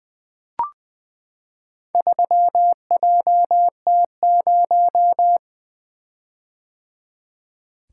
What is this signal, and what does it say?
Morse code, heard as a keyed tone, about 20 wpm, 700 Hz. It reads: 3JT0
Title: Four Letter-Number Combinations - 20wpm